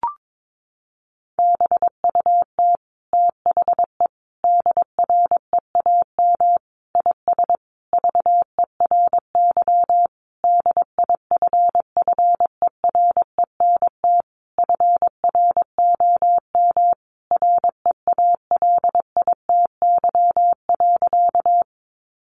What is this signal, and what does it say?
Morse code, heard as a keyed tone, about 22 wpm, 700 Hz. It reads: BUT THE DREAM IS VERY DIFFERENT FROM REALITY.
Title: Sentences from Top 1500 Words 22wpm